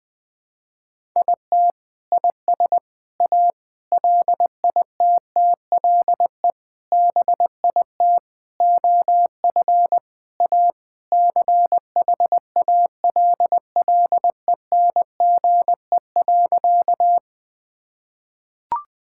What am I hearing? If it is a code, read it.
Morse code, heard as a keyed tone, about 21 wpm, 700 Hz. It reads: IT IS A LITTLE BIT OF A CHALLENGE.
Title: Sentences with 5-Word Phrases - Spoken Beginning - 20wpm